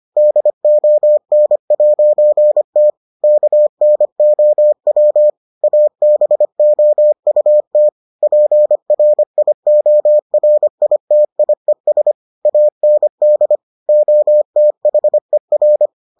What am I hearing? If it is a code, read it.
Morse code, heard as a keyed tone, about 25 wpm, 600 Hz. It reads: DON'T KNOW ABOUT PRIORITIES AND OTHER